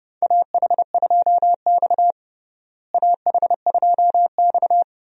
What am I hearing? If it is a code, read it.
Morse code, heard as a keyed tone, about 30 wpm, 700 Hz. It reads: AH2X AH2X
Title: USA Call Signs - 4 Characters - Repeated 2x - 30wpm